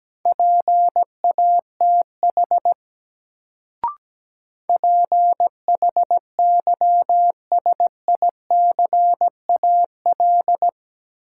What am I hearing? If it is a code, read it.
Morse code, heard as a keyed tone, about 17 wpm, 700 Hz. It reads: PATH PHYSICAL
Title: Sentences from Top 1400 Words - New Word Drill Alt - 17wpm